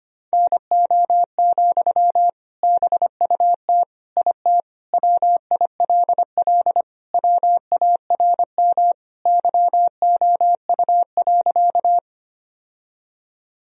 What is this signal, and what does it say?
Morse code, heard as a keyed tone, about 25 wpm, 700 Hz. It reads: NO, BUT IT WILL WARM YOU.
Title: Sentences from Top 1000 Words - Review Part 2 - 25wpm